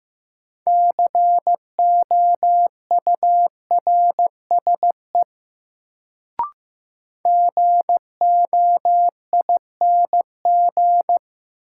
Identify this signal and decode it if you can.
Morse code, heard as a keyed tone, about 15 wpm, 700 Hz. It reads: COURSE GOING
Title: Sentences from Top 500 Words - New Word Drill - 15wpm